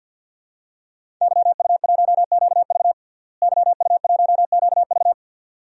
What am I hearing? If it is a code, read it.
Morse code, heard as a keyed tone, about 46 wpm, 700 Hz. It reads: YU1QV YU1QV
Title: DXCC-100 Call Signs - 5 Characters - Repeated 2x - 50wpm